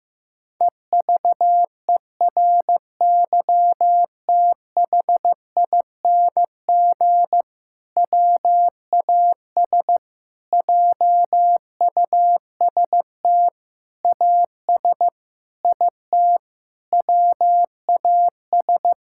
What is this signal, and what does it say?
Morse code, heard as a keyed tone, about 15 wpm, 700 Hz. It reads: EVERYTHING WAS JUST AS IT WAS
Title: Sentences from Top 1600 Words 15wpm